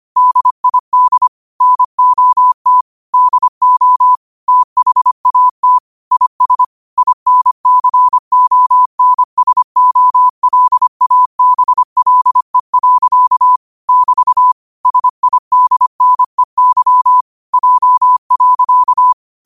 Morse code, heard as a keyed tone, about 25 wpm, 1000 Hz. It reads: DID NOT DO THAT IS INCONSOLABLE. <BT> SIDNEY J.
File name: qotdfiles / 25wpm